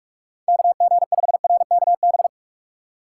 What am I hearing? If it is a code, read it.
Morse code, heard as a keyed tone, about 46 wpm, 700 Hz. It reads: KG5RKB
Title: USA Call Signs - 6 Characters - 4x Character Spacing - 45wpm